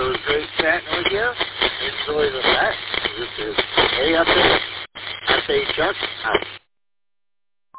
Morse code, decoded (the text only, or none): none